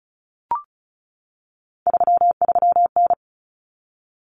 Morse code, heard as a keyed tone, about 34 wpm, 700 Hz. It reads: 33N
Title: Common Cut Numbers - 3-Digit 35wpm